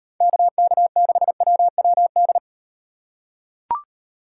Morse code, heard as a keyed tone, about 38 wpm, 700 Hz. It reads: KK6WWD